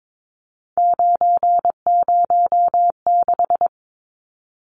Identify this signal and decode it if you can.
Morse code, heard as a keyed tone, about 22 wpm, 700 Hz. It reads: 906
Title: Numbers - 3-Digit 22wpm